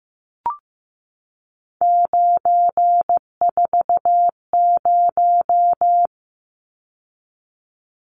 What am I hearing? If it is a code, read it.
Morse code, heard as a keyed tone, about 15 wpm, 700 Hz. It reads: 940
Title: Numbers - 3-Digit 15wpm